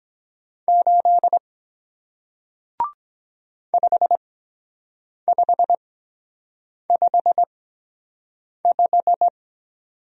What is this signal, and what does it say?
Morse code, heard as a keyed tone, about 23 wpm, spaced as for 7 wpm, 700 Hz. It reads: 8 5555